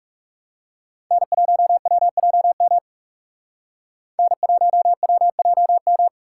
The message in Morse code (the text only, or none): N1WJM N1WJM